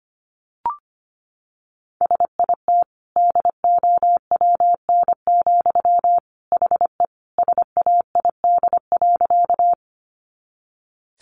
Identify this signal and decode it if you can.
Morse code, heard as a keyed tone, about 25 wpm, 700 Hz. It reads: SIT DOWN, HE SAID.
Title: Sentences from Top 500 Words 25wpm